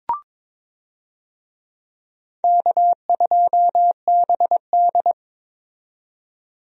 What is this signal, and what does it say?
Morse code, heard as a keyed tone, about 22 wpm, 700 Hz. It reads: K2BD
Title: USA Call Signs - 4 Characters - 22wpm